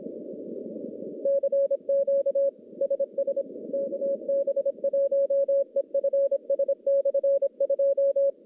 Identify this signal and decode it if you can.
Morse code, heard as a keyed tone, about 26 wpm, 550 Hz. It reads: CQ SS KB1EFS/2
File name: contests / CW-SS-k5tr- -1100